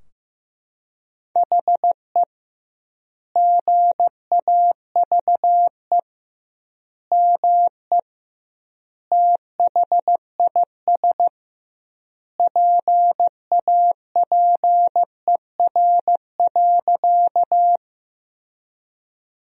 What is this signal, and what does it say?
Morse code, heard as a keyed tone, about 15 wpm, 700 Hz. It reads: HE GAVE ME THIS PAPER.
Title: 2x Word-Spacing - Sentences from Top 500 Words - 15wpm